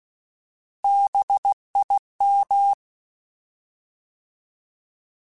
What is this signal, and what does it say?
Morse code, heard as a keyed tone, about 16 wpm, 800 Hz. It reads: BIM
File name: Alphabetical / Alphabetical mp3s / Bs @15